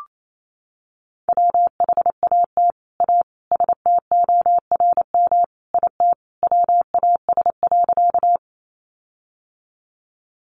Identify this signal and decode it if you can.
Morse code, heard as a keyed tone, about 28 wpm, 700 Hz. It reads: WHAT A STORM IT WAS.
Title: Sentences from Top 1500 Words 28wpm